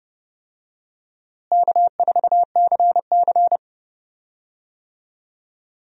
Morse code, heard as a keyed tone, about 31 wpm, 700 Hz. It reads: K4CC
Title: USA Call Signs - 4 Characters - Repeated 3x - 30wpm